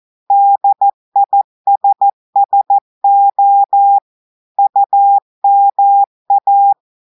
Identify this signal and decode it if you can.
Morse code, heard as a keyed tone, about 14 wpm, 800 Hz. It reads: DISSO UMA
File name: audios / portugais / portugais texte en clair 1 vitesse 14 mots minutes